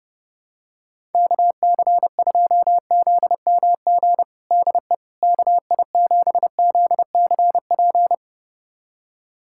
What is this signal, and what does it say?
Morse code, heard as a keyed tone, about 30 wpm, 700 Hz. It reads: KC2ZMG DE KI7ZCP
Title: USA Callsign DE Callsign - 6 Characters - 30wpm